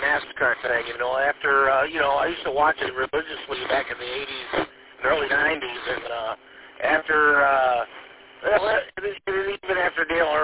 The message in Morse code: none

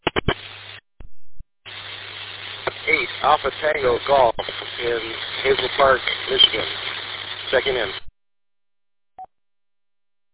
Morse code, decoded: none

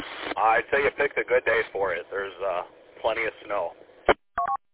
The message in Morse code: none

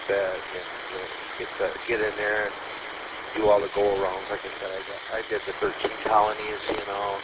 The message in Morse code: none

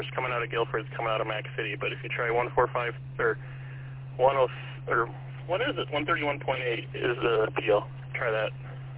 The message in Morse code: none

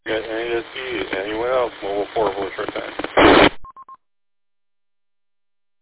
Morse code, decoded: none

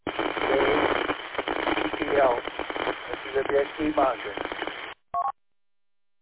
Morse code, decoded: none

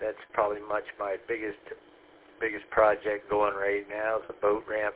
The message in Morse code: none

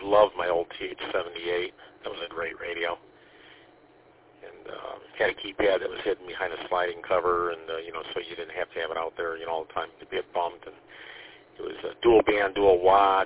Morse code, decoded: none

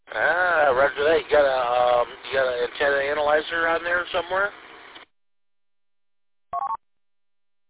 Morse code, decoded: none

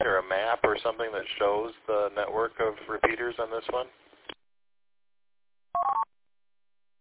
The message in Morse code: none

